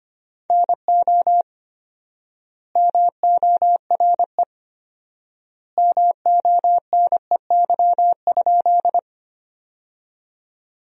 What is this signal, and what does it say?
Morse code, heard as a keyed tone, about 25 wpm, 700 Hz. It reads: NO MORE MONEY?
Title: 4x Word-Spacing - Sentences from Top 400 Words - 25wpm